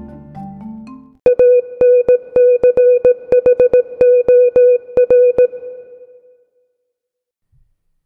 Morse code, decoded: ANCHOR